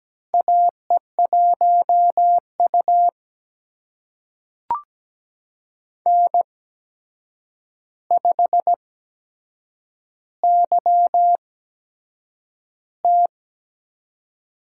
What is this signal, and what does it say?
Morse code, heard as a keyed tone, about 17 wpm, spaced as for 7 wpm, 700 Hz. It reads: AE1U N5YT
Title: USA Call Signs - 4 Characters - 8x Character Spacing - 17wpm